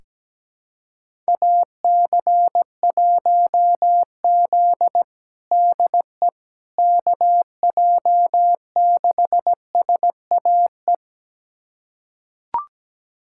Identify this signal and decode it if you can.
Morse code, heard as a keyed tone, about 17 wpm, 700 Hz. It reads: AC1Z DE KJ6SAE
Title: USA Callsign DE Callsign 17wpm